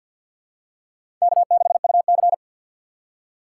Morse code, heard as a keyed tone, about 52 wpm, 700 Hz. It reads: K6UC
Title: USA Call Signs - 4 Characters - 3x Character Spacing - 50wpm